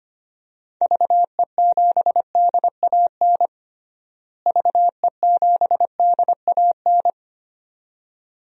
Morse code, heard as a keyed tone, about 25 wpm, 700 Hz. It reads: VE7DAN VE7DAN
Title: DXCC-100 Call Signs - 6 Characters - Repeated 2x - 25wpm